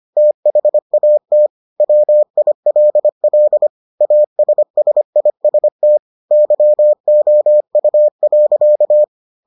Morse code, heard as a keyed tone, about 25 wpm, 600 Hz. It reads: THAT WILL ASSIST YOU.